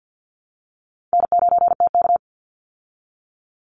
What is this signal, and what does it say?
Morse code, heard as a keyed tone, about 50 wpm, 700 Hz. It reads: N9TK
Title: Easy Call Signs Repeated 2x - 50wpm